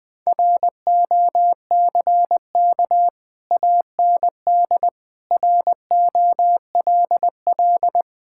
Morse code, heard as a keyed tone, about 20 wpm, 700 Hz. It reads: ROCK AND ROLL
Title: Sets of 6 Words - 20wpm